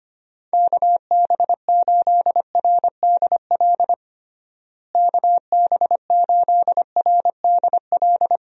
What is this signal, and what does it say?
Morse code, heard as a keyed tone, about 25 wpm, 700 Hz. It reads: KB8RDL KB8RDL
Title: USA Call Signs - 6 Characters - Repeated 2x - 25wpm